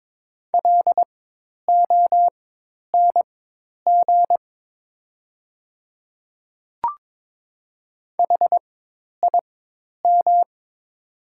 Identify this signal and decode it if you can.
Morse code, heard as a keyed tone, about 22 wpm, spaced as for 10 wpm, 700 Hz. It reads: LONG HIM
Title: 4x Character Spacing - Top 100 Words - 22wpm